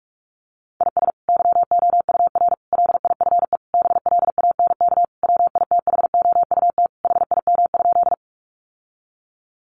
Morse code, heard as a keyed tone, about 44 wpm, 700 Hz. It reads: IS YOUR LIFE BLANK WITHOUT HIM?
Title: Sentences from Top 1500 Words 45wpm